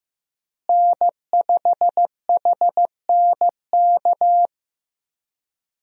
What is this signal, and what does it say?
Morse code, heard as a keyed tone, about 16 wpm, 700 Hz. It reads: N5HNK
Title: USA Call Signs - 5 Characters - Repeated 2x - 15wpm